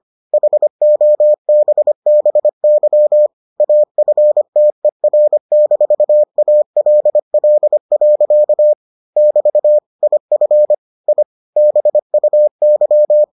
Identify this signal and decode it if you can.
Morse code, heard as a keyed tone, about 25 wpm, 600 Hz. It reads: HOBBY AFTER-ALL. <BT> IF I BUY